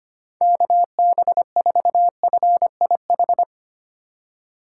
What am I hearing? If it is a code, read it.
Morse code, heard as a keyed tone, about 27 wpm, 700 Hz. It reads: KB4FIH
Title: Speed Racing - USA 2x3 Call Signs 25wpm